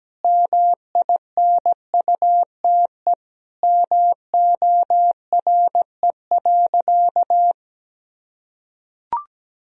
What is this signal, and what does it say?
Morse code, heard as a keyed tone, about 17 wpm, 700 Hz. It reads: MINUTE MORE.
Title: Sentences from Top 1000 Words - Review Part 4 - 17wpm